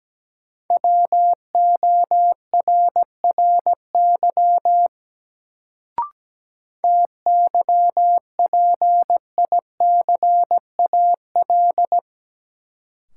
Morse code, heard as a keyed tone, about 17 wpm, 700 Hz. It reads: WORRY TYPICAL
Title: Sentences from Top 1100 Words - New Word Drill - 17wpm